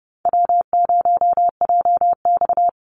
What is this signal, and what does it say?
Morse code, heard as a keyed tone, about 30 wpm, 700 Hz. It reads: W0JX
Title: Easy Call Signs Repeated 3x - 30wpm